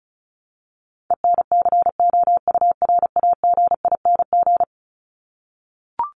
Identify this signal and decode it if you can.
Morse code, heard as a keyed tone, about 34 wpm, 700 Hz. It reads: ENCOURAGING
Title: Top 1751-2000 Words 35wpm